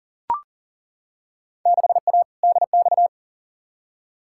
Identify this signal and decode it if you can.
Morse code, heard as a keyed tone, about 40 wpm, 700 Hz. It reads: 6A DX
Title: Field Day Exchanges Encore - 40wpm